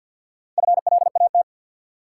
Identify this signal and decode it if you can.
Morse code, heard as a keyed tone, about 53 wpm, 700 Hz. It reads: FLAT
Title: Sentences from Top 1000 Words - New Word Drill - 50wpm